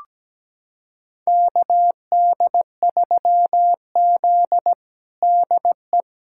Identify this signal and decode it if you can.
Morse code, heard as a keyed tone, about 17 wpm, 700 Hz. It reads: KD3Z DE
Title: USA Callsign DE Callsign - 4 Characters - 17wpm